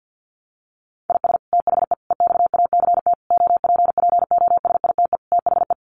Morse code, heard as a keyed tone, about 50 wpm, 700 Hz. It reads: IS THE EXACT OPPOSITE THE
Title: Sentences from Top 1300 Words 50wpm